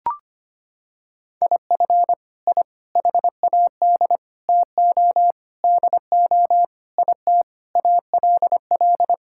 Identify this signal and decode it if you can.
Morse code, heard as a keyed tone, about 25 wpm, 700 Hz. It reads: IF I HAD TO DO IT ALL
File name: practice / 25wpm / Sentences with 9-Word Phrases - 25wpm